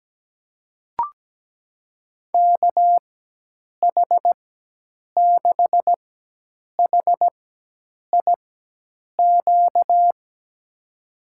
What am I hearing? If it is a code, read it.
Morse code, heard as a keyed tone, about 17 wpm, spaced as for 8 wpm, 700 Hz. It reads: KH6HIQ
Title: USA Call Signs - 6 Characters - 4x Character Spacing - 17wpm